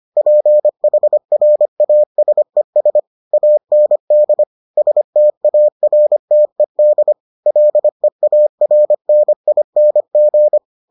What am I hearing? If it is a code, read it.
Morse code, heard as a keyed tone, about 25 wpm, 600 Hz. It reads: PHRASES AND STARTED LEARNING